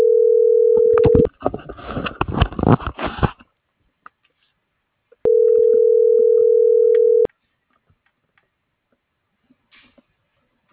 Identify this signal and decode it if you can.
Ringback tone (North America), 440 + 480 Hz.